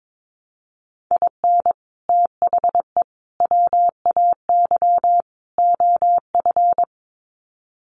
Morse code, heard as a keyed tone, about 22 wpm, 700 Hz. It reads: IN THE WAY OF
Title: Sets of 4 Words 22wpm